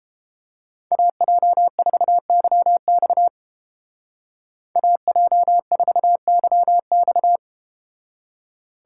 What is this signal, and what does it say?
Morse code, heard as a keyed tone, about 30 wpm, 700 Hz. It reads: AJ4YX AJ4YX